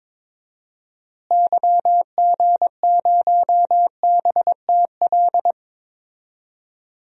Morse code, heard as a keyed tone, about 22 wpm, 700 Hz. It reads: YG0BTL
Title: DXCC-100 Call Signs - 6 Characters - Repeated 2x - 22wpm